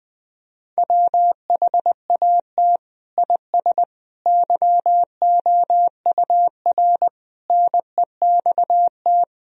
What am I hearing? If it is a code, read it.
Morse code, heard as a keyed tone, about 21 wpm, 700 Hz. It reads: WHAT IS YOUR NEXT
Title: Sentences from Top 1000 Words - Review Part 1 - 20wpm